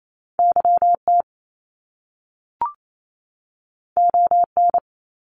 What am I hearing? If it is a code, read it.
Morse code, heard as a keyed tone, about 28 wpm, 700 Hz. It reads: YT ON
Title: Canadian Provinces and Territories 28wpm